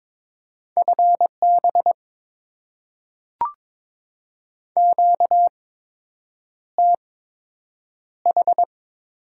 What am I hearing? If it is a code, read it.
Morse code, heard as a keyed tone, about 22 wpm, spaced as for 8 wpm, 700 Hz. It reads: FB QTH
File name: Top 20 QSO Elements - 8x Character Spacing - 22wpm